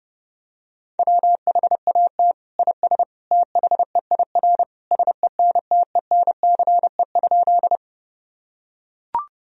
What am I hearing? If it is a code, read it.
Morse code, heard as a keyed tone, about 30 wpm, 700 Hz. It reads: WHAT IS THEIR SENTENCE?
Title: Sentences from Top 1000 Words - Review Part 1 - 30wpm